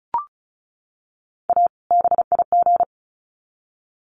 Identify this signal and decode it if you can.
Morse code, heard as a keyed tone, about 34 wpm, 700 Hz. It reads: A BIG
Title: Sets of 2 Words 35wpm